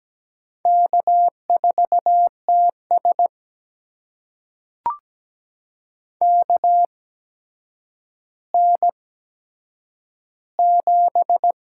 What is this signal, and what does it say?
Morse code, heard as a keyed tone, about 17 wpm, spaced as for 8 wpm, 700 Hz. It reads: K4TS KN7
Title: USA Call Signs - 8x Character Spacing - 17wpm